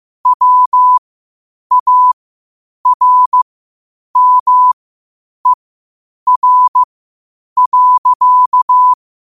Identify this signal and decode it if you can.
Morse code, heard as a keyed tone, about 15 wpm, spaced as for 8 wpm, 1000 Hz. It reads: WARMER.
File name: qotdfiles / 8wpm